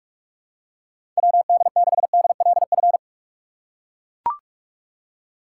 Morse code, heard as a keyed tone, about 42 wpm, 700 Hz. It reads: WD6DLF
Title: USA Call Signs - 4x Character Spacing - 45wpm